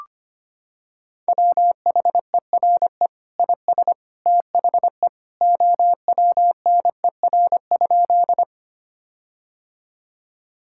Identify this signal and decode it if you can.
Morse code, heard as a keyed tone, about 25 wpm, 700 Hz. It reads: WHERE IS THE OWNER?
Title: Sentences from Top 1600 Words 25wpm